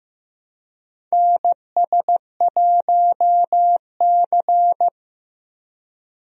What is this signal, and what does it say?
Morse code, heard as a keyed tone, about 15 wpm, 700 Hz. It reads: NS1C
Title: USA Call Signs - 4 Characters - Repeated 3x - 15wpm